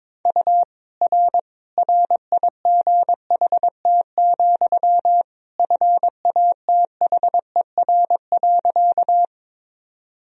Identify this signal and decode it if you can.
Morse code, heard as a keyed tone, about 22 wpm, 700 Hz. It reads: U R RIGHT, FATHER.
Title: Sentences from Top 300 Words and Abbreviations 22wpm